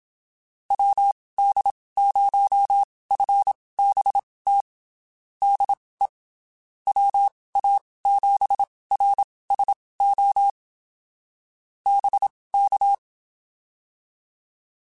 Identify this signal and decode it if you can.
Morse code, heard as a keyed tone, about 27 wpm, spaced as for 19 wpm, 800 Hz. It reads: WD0FBT DE WA7RSO BK
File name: (spaced) / QSOs Spaced-mp3s / WD0FBT 1 @25 sp